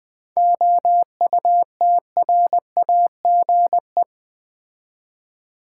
Morse code, heard as a keyed tone, about 20 wpm, 700 Hz. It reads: OUTRAGE
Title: 7 Letter Words - First 3 Letters Given - 20wpm